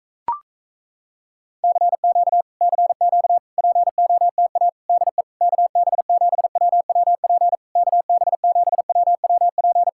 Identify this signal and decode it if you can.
Morse code, heard as a keyed tone, about 38 wpm, 700 Hz. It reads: CQ CQ POTA DE KB7WWP KB7WWP